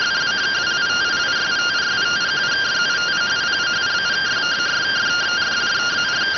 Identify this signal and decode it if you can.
Two-tone teleprinter signal, about 170 Hz shift.